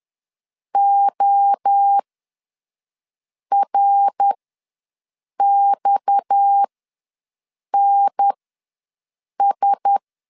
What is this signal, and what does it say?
Morse code, heard as a keyed tone, about 10 wpm, spaced as for 5 wpm, 800 Hz. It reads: ORXNS
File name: LEZIONI / 7 GF9Z / 7 GF ws 10wpm 3918